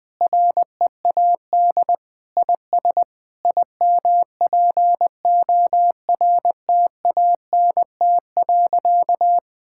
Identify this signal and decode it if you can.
Morse code, heard as a keyed tone, about 21 wpm, 700 Hz. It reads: READ IS IMPORTANT.
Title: Sentences with 3-Word Phrases - Spoken Beginning - 20wpm